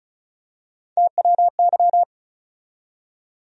Morse code, heard as a keyed tone, about 34 wpm, 700 Hz. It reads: TWY
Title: USA Call Sign Postfixes - 3 Letters 35wpm